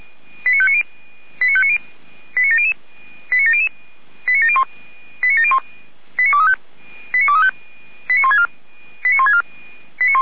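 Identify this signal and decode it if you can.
Five-tone selective call (ZVEI), heard as quick runs of five tones, tones of about 70 ms. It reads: 89590 89590 89700 89700 89701 89701 89235 89235 89164 89164
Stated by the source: ZVEI five-tone selective call received on SDR.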